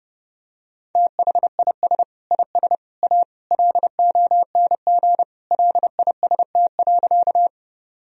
Morse code, heard as a keyed tone, about 30 wpm, 700 Hz. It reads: THIS IS A LONG LIST.
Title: Sentences from Top 1000 Words - Review Part 2 - 30wpm